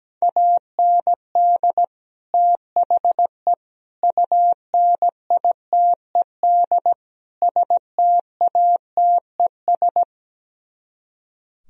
Morse code, heard as a keyed tone, about 17 wpm, 700 Hz. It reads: AND THE UNITED STATES
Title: Speed-Racing - Sets of 4 Words 17wpm